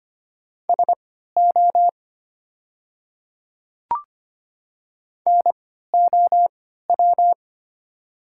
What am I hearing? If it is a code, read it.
Morse code, heard as a keyed tone, about 25 wpm, spaced as for 14 wpm, 700 Hz. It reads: SO NOW
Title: 3x Character Spacing - Top 100 Words - 25wpm